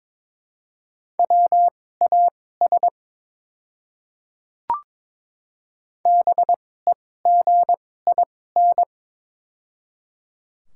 Morse code, heard as a keyed tone, about 22 wpm, spaced as for 16 wpm, 700 Hz. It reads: WAS BEGIN
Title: 2x Character Spacing - Top 300 Words - 22wpm